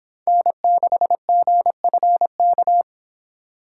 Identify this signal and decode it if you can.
Morse code, heard as a keyed tone, about 30 wpm, 700 Hz. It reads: N6GFK